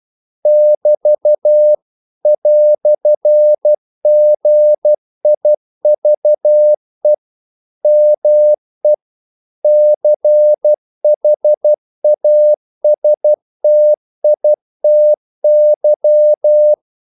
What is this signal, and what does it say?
Morse code, heard as a keyed tone, about 12 wpm, 600 Hz. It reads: <BT> "GIVE ME CHASTITY
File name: CW / Practice / EnglishQuotations0112